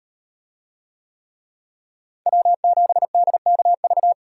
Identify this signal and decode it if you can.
Morse code, heard as a keyed tone, about 38 wpm, 700 Hz. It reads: W7DKV